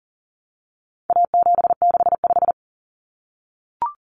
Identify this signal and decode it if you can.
Morse code, heard as a keyed tone, about 40 wpm, 700 Hz. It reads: A765
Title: Common Cut Numbers - 4-Digit 40wpm